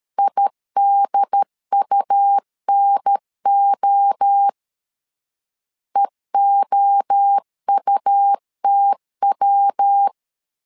Morse code, heard as a keyed tone, about 13 wpm, 800 Hz. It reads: IDUNO EOUTW
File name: LEZIONI / 3 MWRO / 3 MWRO 12wpm 1730